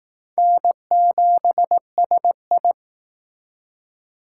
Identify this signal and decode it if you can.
Morse code, heard as a keyed tone, about 20 wpm, 700 Hz. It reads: N7SI